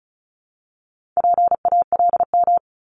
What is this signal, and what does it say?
Morse code, heard as a keyed tone, about 36 wpm, 700 Hz. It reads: PALM
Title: Multi-letter Phonemes 35wpm